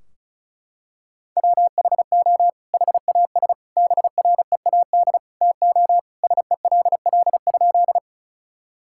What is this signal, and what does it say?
Morse code, heard as a keyed tone, about 36 wpm, 700 Hz. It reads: WHO HAS BREAD TO SELL?
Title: Sentences from Top 1600 Words 35wpm